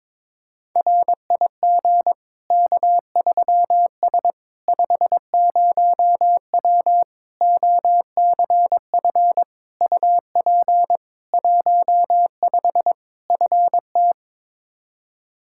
Morse code, heard as a keyed tone, about 22 wpm, 700 Hz. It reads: RIG K3S 50W OCF UP 15 FT
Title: Practice QSO - Exchange 2 - Info - 22wpm